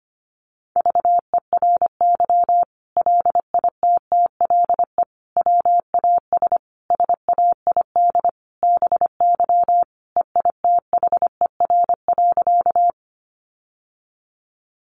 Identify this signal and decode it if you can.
Morse code, heard as a keyed tone, about 25 wpm, 700 Hz. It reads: VERY LITTLE WAS SAID BY EITHER.
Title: Sentences from Top 600 Words 25wpm